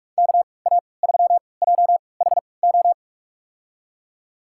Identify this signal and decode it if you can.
Morse code, heard as a keyed tone, about 46 wpm, spaced as for 25 wpm, 690 Hz. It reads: KA3JHO
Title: USA Call Signs - 3x Character Spacing - 45wpm